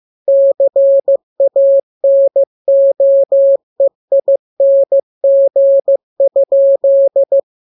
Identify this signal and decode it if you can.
Morse code, heard as a keyed tone, about 15 wpm, 550 Hz. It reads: CANOEING?
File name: jokes / joke 1 15 0000